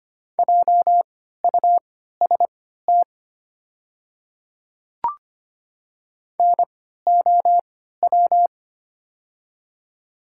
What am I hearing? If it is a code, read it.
Morse code, heard as a keyed tone, about 25 wpm, spaced as for 14 wpm, 700 Hz. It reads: JUST NOW
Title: 3x Character Spacing - Top 300 Words - 25wpm